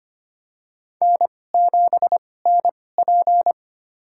Morse code, heard as a keyed tone, about 25 wpm, spaced as for 18 wpm, 700 Hz. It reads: N7NP
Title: USA Call Signs - 4 Characters - 2x Character Spacing - 25wpm